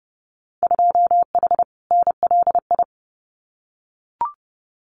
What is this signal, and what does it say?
Morse code, heard as a keyed tone, about 30 wpm, 700 Hz. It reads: 2H NLI
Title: Winter Field Day Exchanges Encore 30wpm